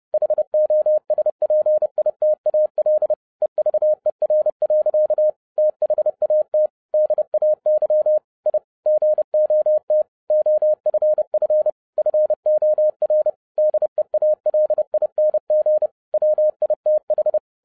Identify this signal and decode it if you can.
Morse code, heard as a keyed tone, about 30 wpm, 610 Hz. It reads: HOSPITAL EVER. THAT DAY I GOT OFF FOR DEALING WITH
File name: 2011-07-mp3 / n55 BACK FROM A BIG ACCIDENT 30wpm